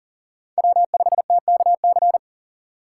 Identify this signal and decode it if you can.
Morse code, heard as a keyed tone, about 40 wpm, 700 Hz. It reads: W5TKC
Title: USA Call Signs - 5 Characters - 6x Character Spacing - 40wpm